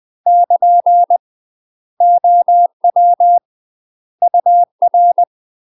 Morse code, heard as a keyed tone, about 20 wpm, 700 Hz. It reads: <KN> OW UR
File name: UK MASTER QSOs TRAINING / 20WPM / UK MASTER QSOs 0049